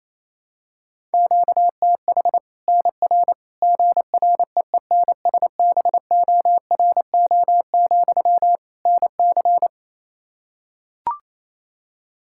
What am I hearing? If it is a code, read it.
Morse code, heard as a keyed tone, about 28 wpm, 700 Hz. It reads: QTH NR GREENSBORO, NC
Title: Practice QSO - Exchange 1 - Beginning - 28wpm